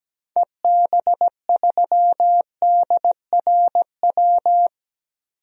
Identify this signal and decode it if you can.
Morse code, heard as a keyed tone, about 17 wpm, 700 Hz. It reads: EB3DRW
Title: DXCC-100 Call Signs - 6 Characters - 17wpm